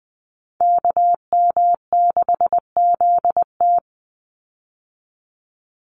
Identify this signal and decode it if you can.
Morse code, heard as a keyed tone, about 20 wpm, 700 Hz. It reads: KM6ZT
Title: Call Signs - 20wpm